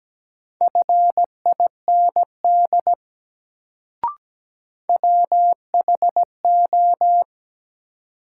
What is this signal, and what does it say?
Morse code, heard as a keyed tone, about 17 wpm, 700 Hz. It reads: FIND WHO
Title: Sets of 2 Words - New Word Drill - 17wpm